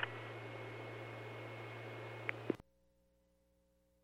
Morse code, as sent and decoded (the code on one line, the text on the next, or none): none